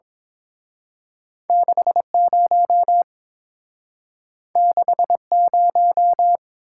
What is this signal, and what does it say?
Morse code, heard as a keyed tone, about 30 wpm, 700 Hz. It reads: 60 60